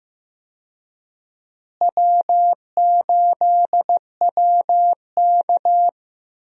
Morse code, heard as a keyed tone, about 15 wpm, 700 Hz. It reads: W8WK
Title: USA Call Signs - 4 Characters - Repeated 2x - 15wpm